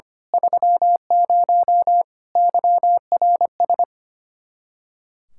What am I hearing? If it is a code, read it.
Morse code, heard as a keyed tone, about 25 wpm, 700 Hz. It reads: 30 YRS
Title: Speed-Racing Common QSO Phrases - 25wpm